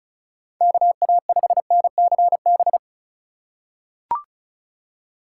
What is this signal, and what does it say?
Morse code, heard as a keyed tone, about 34 wpm, 700 Hz. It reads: KA5NCB
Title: USA Call Signs - 6 Characters - 4x Character Spacing - 35wpm